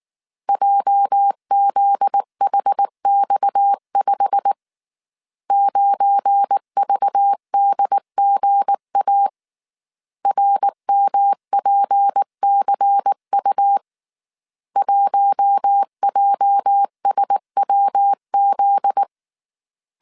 Morse code, heard as a keyed tone, about 19 wpm, 800 Hz. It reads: JZHX5 9VDGA RMPCU 1JSWZ
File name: CW / DET / 1 9 1500C S A 18wpm 2459